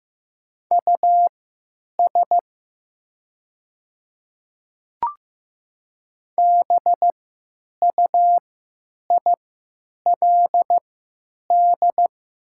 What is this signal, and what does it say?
Morse code, heard as a keyed tone, about 15 wpm, spaced as for 9 wpm, 700 Hz. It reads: US BUILD
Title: 3x Character Spacing - Top 300 Words - 15wpm